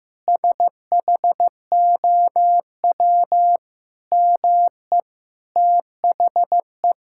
Morse code, heard as a keyed tone, about 15 wpm, 700 Hz. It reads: SHOW ME THE
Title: Sentences from Top 1000 Words - Review Part 6 - 15wpm